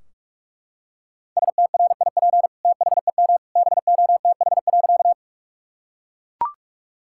Morse code, heard as a keyed tone, about 41 wpm, 700 Hz. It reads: STRIP THEM BOTH.
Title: Sentences from Top 1600 Words 45wpm